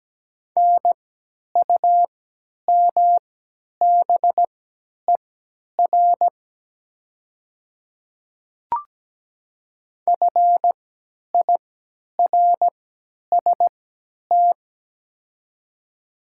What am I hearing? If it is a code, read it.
Morse code, heard as a keyed tone, about 17 wpm, spaced as for 10 wpm, 700 Hz. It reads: NUMBER FIRST
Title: 3x Character Spacing - Top 200 Words - 17wpm